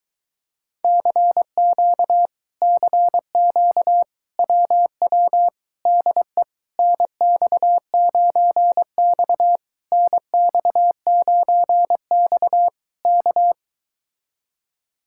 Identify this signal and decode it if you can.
Morse code, heard as a keyed tone, about 23 wpm, 700 Hz. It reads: CQ CQ WW DE NX9X NX9X K